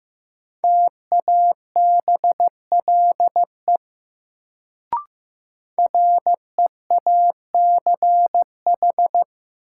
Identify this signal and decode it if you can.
Morse code, heard as a keyed tone, about 15 wpm, 700 Hz. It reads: TABLE REACH
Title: Sentences from Top 400 Words - New Word Drill - 15wpm